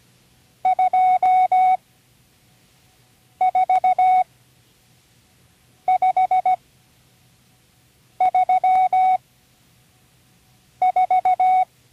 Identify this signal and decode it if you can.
Morse code, heard as a keyed tone, about 17 wpm, spaced as for 5 wpm, 740 Hz. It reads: 24534
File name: CW Learning Programs / Learning Morse Code (from Tune in the World) / 35 answers